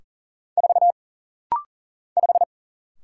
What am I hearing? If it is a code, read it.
Morse code, heard as a keyed tone, about 40 wpm, 700 Hz. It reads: 4 5
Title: Single Letter-Number Alt - Rapid-Fire 40wpm